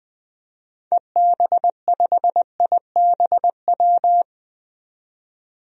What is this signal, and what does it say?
Morse code, heard as a keyed tone, about 20 wpm, 700 Hz. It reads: EB5IBW
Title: DXCC-100 Call Signs - 6 Characters - 20wpm